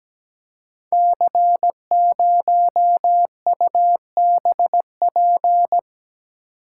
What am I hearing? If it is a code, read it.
Morse code, heard as a keyed tone, about 17 wpm, 700 Hz. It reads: C0UBP
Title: Code Groups - 5 Letter:Numbers - 17wpm